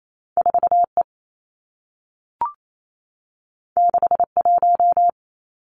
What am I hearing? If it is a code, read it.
Morse code, heard as a keyed tone, about 28 wpm, 700 Hz. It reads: 4 E 6 1
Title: Two Challenging Letter-Number Combinations 28wpm